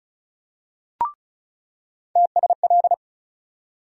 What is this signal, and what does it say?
Morse code, heard as a keyed tone, about 35 wpm, 700 Hz. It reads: TSL
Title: USA Call Sign Postfixes - 3 Letters 35wpm